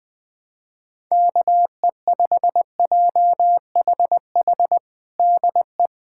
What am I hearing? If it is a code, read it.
Morse code, heard as a keyed tone, about 21 wpm, 700 Hz. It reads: KE5JHH DE
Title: USA Callsign DE Callsign - 6 Characters - 20wpm